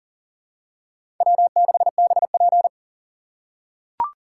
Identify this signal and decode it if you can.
Morse code, heard as a keyed tone, about 40 wpm, 700 Hz. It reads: W6BP
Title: USA Call Signs - 3x Character Spacing - 40wpm